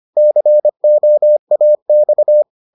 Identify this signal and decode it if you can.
Morse code, heard as a keyed tone, about 25 wpm, 600 Hz. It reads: COAX